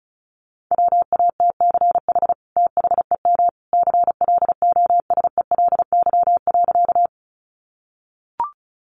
Morse code, heard as a keyed tone, about 34 wpm, 700 Hz. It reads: WATCH THEM CLOSELY.
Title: Sentences from Top 1400 Words 35wpm